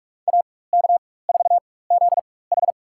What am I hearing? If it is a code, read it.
Morse code, heard as a keyed tone, about 47 wpm, spaced as for 21 wpm, 700 Hz. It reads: AK4ZH
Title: USA Call Signs - 5 Characters - 4x Character Spacing - 45wpm